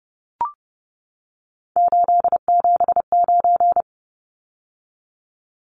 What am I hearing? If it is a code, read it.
Morse code, heard as a keyed tone, about 30 wpm, 700 Hz. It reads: 879
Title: Numbers - 3-Digit 30wpm